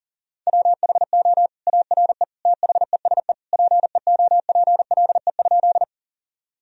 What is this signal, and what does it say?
Morse code, heard as a keyed tone, about 40 wpm, 700 Hz. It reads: WHO ARE THESE PEOPLE?
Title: Sentences from Top 1000 Words - Review Part 1 - 40wpm